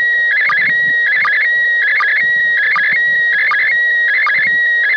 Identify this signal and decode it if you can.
SSTV (slow-scan television), PD 180.
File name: SSTV / ISS2015AprilPD180x